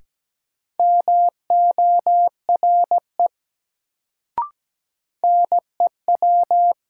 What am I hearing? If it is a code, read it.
Morse code, heard as a keyed tone, about 17 wpm, 700 Hz. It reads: MORE NEW
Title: Sets of 2 Words - New Word Drill - 17wpm